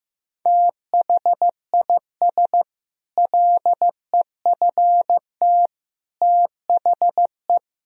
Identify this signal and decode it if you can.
Morse code, heard as a keyed tone, about 15 wpm, 700 Hz. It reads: THIS LEFT THE
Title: Sentences from Top 1000 Words - Review Part 4 - 15wpm